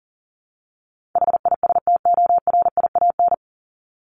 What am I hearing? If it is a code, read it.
Morse code, heard as a keyed tone, about 40 wpm, 700 Hz. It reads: HISTORIAN
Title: Top 1751-2000 Words 40wpm